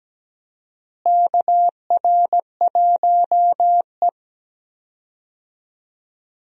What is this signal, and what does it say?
Morse code, heard as a keyed tone, about 17 wpm, 700 Hz. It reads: KR1E
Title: USA Call Signs - 4 Characters - Repeated 2x - 17wpm